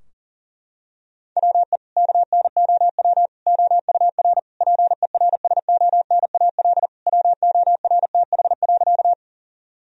Morse code, heard as a keyed tone, about 40 wpm, 700 Hz. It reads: WE KNOW OUR PERSONAL WORTH.
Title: Sentences from Top 1000 Words - Review Part 4 - 40wpm